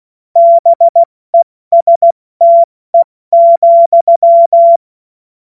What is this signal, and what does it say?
Morse code, heard as a keyed tone, about 16 wpm, spaced as for 14 wpm, 680 Hz. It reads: BESTE,
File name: Morsekurs / Witze / Witz 16. Februar 2021, 70 F80